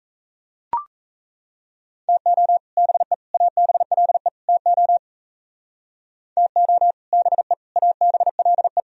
Morse code, heard as a keyed tone, about 33 wpm, 700 Hz. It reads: TO BE ABLE TO TO BE ABLE
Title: Speed-Racing - Sets of 4 Words 28wpm